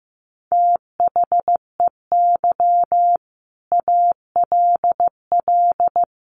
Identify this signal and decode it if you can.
Morse code, heard as a keyed tone, about 15 wpm, 700 Hz. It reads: THEY ALL
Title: Sentences from Top 500 Words 15wpm